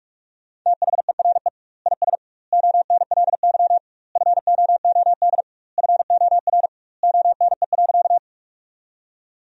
Morse code, heard as a keyed tone, about 46 wpm, 700 Hz. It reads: THERE IS ONLY FOOD FOR ONE.
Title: 2x Word-Spacing - Sentences from Top 300 Words - 45wpm